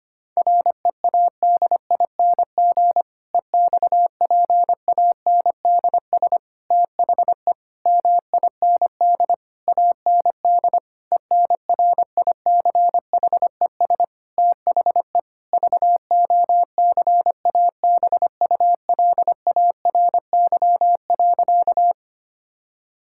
Morse code, heard as a keyed tone, about 25 wpm, 700 Hz. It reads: READING EXPANDS THE MIND AND ENRICHES THE VOCABULARY.
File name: practice / 25wpm / Difficult Sentences with Hints - 25wpm